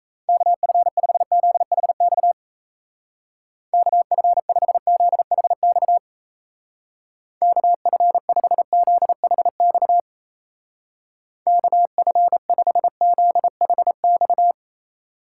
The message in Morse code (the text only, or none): KF5ZHX KF5ZHX KF5ZHX KF5ZHX